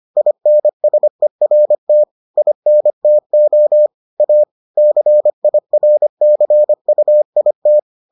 Morse code, heard as a keyed tone, about 25 wpm, 600 Hz. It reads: INSERT INTO A CIRCUIT